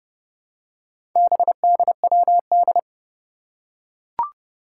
Morse code, heard as a keyed tone, about 30 wpm, 700 Hz. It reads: BDWD
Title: Four Challenging Letter-Number Combinations 30wpm